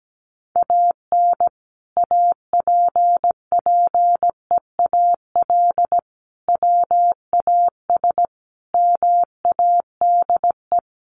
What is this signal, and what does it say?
Morse code, heard as a keyed tone, about 17 wpm, 700 Hz. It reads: AN APPEAL WAS MADE
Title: Sentences from Top 1300 Words 17wpm